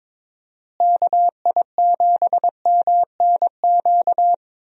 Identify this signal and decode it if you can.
Morse code, heard as a keyed tone, about 22 wpm, 700 Hz. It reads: KI7MNQ
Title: USA Call Signs - 6 Characters - Repeated 2x - 22wpm